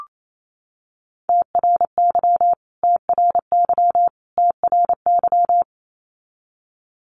Morse code, heard as a keyed tone, about 28 wpm, 700 Hz. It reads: TRY TRY TRY
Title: Top 500 Words Repeated 28wpm